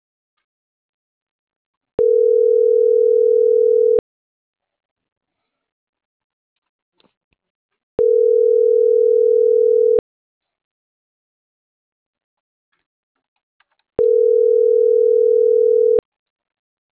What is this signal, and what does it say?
Ringback tone (North America), 440 + 480 Hz.